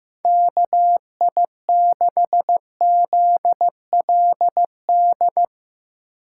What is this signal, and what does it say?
Morse code, heard as a keyed tone, about 15 wpm, 700 Hz. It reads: KI6ZLD
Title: USA Call Signs - 6 Characters - 3x Character Spacing - 15wpm